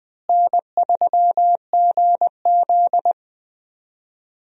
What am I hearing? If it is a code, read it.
Morse code, heard as a keyed tone, about 20 wpm, 700 Hz. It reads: N3GZ
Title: USA Call Signs - 4 Characters - Repeated 2x - 20wpm